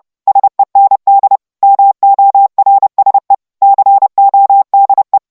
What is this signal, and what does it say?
Morse code, heard as a keyed tone, about 29 wpm, 790 Hz. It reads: SEND MORSE CODE